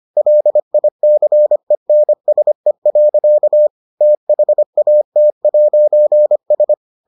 Morse code, heard as a keyed tone, about 25 wpm, 600 Hz. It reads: LICENSE. THAT'S